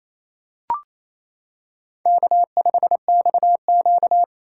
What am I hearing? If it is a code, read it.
Morse code, heard as a keyed tone, about 28 wpm, 700 Hz. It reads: K5XQ
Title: K1USN SST QSO Practice - 28wpm